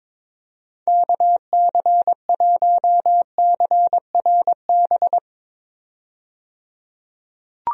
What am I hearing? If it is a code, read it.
Morse code, heard as a keyed tone, about 22 wpm, 700 Hz. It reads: KC1CRB
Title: USA Call Signs - 6 Characters - 22wpm